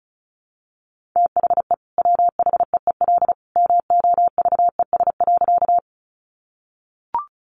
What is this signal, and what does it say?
Morse code, heard as a keyed tone, about 34 wpm, 700 Hz. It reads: THE WHEEL MOVES.
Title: Additional Easy Phrases 35wpm